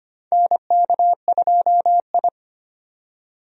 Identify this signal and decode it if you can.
Morse code, heard as a keyed tone, about 25 wpm, 700 Hz. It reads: NK2I
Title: USA Call Signs - 4 Characters - 25wpm